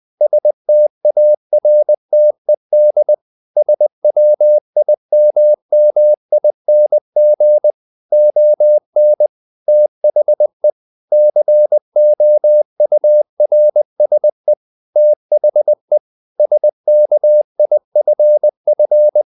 Morse code, heard as a keyed tone, about 20 wpm, 600 Hz. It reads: STARTED SWIMMING ON THE COURSE THE SKIFF
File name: the old man and the sea part 2 20wpm